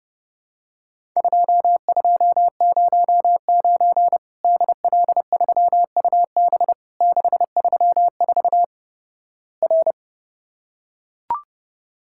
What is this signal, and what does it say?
Morse code, heard as a keyed tone, about 31 wpm, 700 Hz. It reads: 2209 DL3UB 634 R
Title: Worked All Europe DX Contest - QTC Exchange - 30wpm